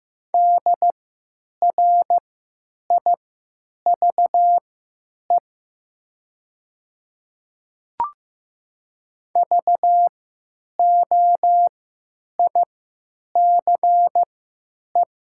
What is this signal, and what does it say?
Morse code, heard as a keyed tone, about 15 wpm, spaced as for 8 wpm, 700 Hz. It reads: DRIVE VOICE
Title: 3x Character Spacing - Top 500 Words - 15wpm